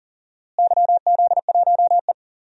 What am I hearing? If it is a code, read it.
Morse code, heard as a keyed tone, about 40 wpm, 700 Hz. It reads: YZ1E
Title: DXCC-100 Call Signs - 4 Characters - Repeated 3x - 40wpm